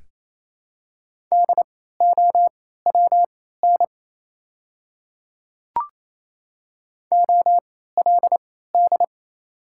Morse code, heard as a keyed tone, about 28 wpm, spaced as for 16 wpm, 700 Hz. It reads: DOWN OLD
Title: 3x Character Spacing - Top 300 Words - 28wpm